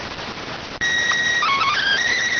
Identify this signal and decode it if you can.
SSTV (slow-scan television), PD 290.